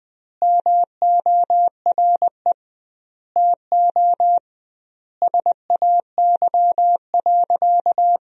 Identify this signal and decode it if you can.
Morse code, heard as a keyed tone, about 20 wpm, 700 Hz. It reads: MORE TO SAY.
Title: 2x Word-Spacing - Sentences from Top 500 Words - 20wpm